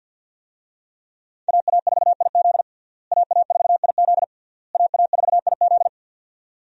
Morse code, heard as a keyed tone, about 50 wpm, 700 Hz. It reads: AA4IZ AA4IZ AA4IZ
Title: USA Call Signs - 5 Characters - Repeated 3x - 50wpm